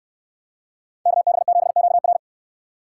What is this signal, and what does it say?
Morse code, heard as a keyed tone, about 66 wpm, 690 Hz. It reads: KB7QR